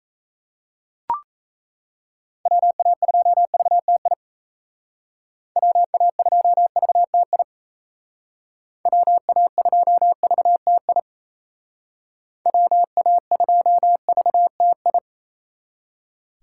Morse code, heard as a keyed tone, about 34 wpm, 700 Hz. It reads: WA2VTI WA2VTI WA2VTI WA2VTI
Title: Speed Racing - USA 2x3 Call Signs 28wpm